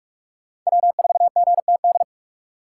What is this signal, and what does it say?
Morse code, heard as a keyed tone, about 41 wpm, 700 Hz. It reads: W4GTD